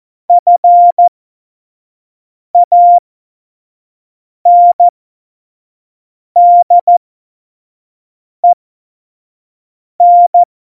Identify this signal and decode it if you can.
Morse code, heard as a keyed tone, about 14 wpm, spaced as for 5 wpm, 700 Hz. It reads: FANDEN
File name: Morsekurs / Witze / Witz 29. Oktober 2019, 25 F70